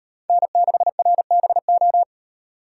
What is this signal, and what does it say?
Morse code, heard as a keyed tone, about 38 wpm, 700 Hz. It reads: N6RBO